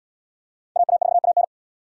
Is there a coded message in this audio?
Yes: Morse code from 0.8 to 1.4 s, heard as a keyed tone, about 68 wpm, 700 Hz. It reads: AA4AN